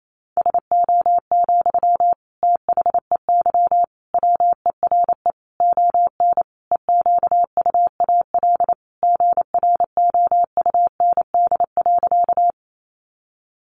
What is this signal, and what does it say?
Morse code, heard as a keyed tone, about 28 wpm, 700 Hz. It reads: SO, THEY WERE ON EQUAL GROUND.
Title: Sentences from Top 1400 Words 28wpm